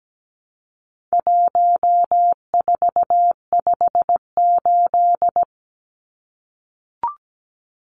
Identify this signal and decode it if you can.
Morse code, heard as a keyed tone, about 17 wpm, 700 Hz. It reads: 1458
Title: Numbers - 4-Digit 17wpm